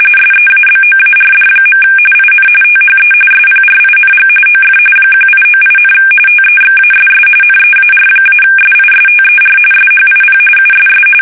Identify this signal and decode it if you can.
RTTY, about 45.45 baud. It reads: SEABOARD. IN THE SPRING OF 1898, TEXASS NEAR-SISTER SHIP USS
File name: fsk850 / battleships